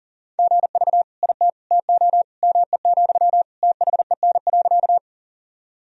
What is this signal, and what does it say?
Morse code, heard as a keyed tone, about 37 wpm, 700 Hz. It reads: GV IT TO ME, THEN.
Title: Sentences from Top 200 Words and Abbreviations 40wpm